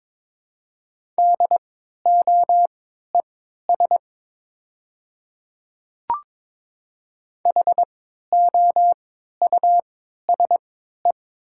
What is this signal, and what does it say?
Morse code, heard as a keyed tone, about 23 wpm, spaced as for 13 wpm, 700 Hz. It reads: DOES HOUSE
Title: 3x Character Spacing - Top 500 Words - 22wpm